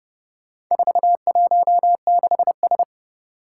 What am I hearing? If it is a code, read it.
Morse code, heard as a keyed tone, about 30 wpm, 700 Hz. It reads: 416S
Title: Four Challenging Letter-Number Combinations 30wpm